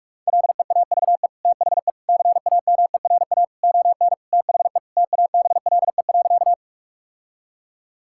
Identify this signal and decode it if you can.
Morse code, heard as a keyed tone, about 41 wpm, 700 Hz. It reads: LEAVE THE CAMERA ON THE TABLE.
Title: Sentences from Top 1000 Words - Review Part 5 - 45wpm